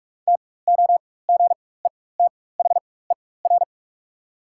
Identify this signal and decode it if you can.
Morse code, heard as a keyed tone, about 44 wpm, spaced as for 21 wpm, 690 Hz. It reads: TOGETHER
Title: 4x Character Spacing - Top 300 Words - 45wpm